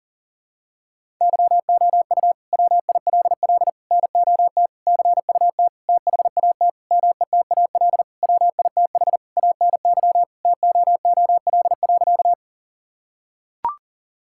Morse code, heard as a keyed tone, about 40 wpm, 700 Hz. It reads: YOU WILL NOT CUT THAT METAL WITH ANY TOOL.
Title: Sentences from Top 1000 Words - Review Part 6 - 40wpm